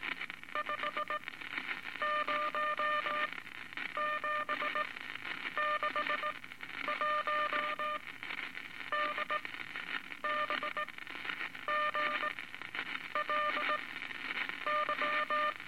Morse code, heard as a keyed tone, about 19 wpm, spaced as for 9 wpm, 1200 Hz, with noise. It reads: 50761 DBGLY
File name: Morse Code Records / 17A LCM 162 Eight Words Per Minute